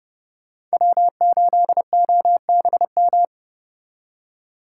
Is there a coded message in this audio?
Yes: Morse code from 0.7 to 3.2 s, heard as a keyed tone, about 34 wpm, 700 Hz. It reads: W8OBM